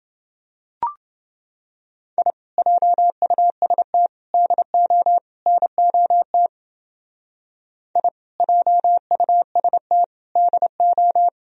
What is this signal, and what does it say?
Morse code, heard as a keyed tone, about 29 wpm, 700 Hz. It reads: I JUST DO NOT I JUST DO